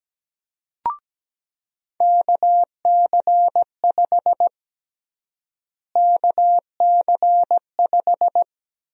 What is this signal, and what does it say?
Morse code, heard as a keyed tone, about 17 wpm, 700 Hz. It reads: KC5 KC5
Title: USA Call Sign Prefixes - Repeated 2x - 17wpm